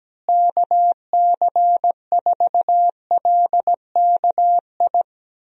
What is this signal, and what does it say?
Morse code, heard as a keyed tone, about 17 wpm, 700 Hz. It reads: KC4LKI
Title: USA Call Signs - 6 Characters - Repeated 3x - 17wpm